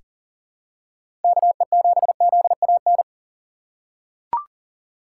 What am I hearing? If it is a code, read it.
Morse code, heard as a keyed tone, about 40 wpm, 700 Hz. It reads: KE7ZAN
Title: USA Call Signs - 6 Characters - 4x Character Spacing - 40wpm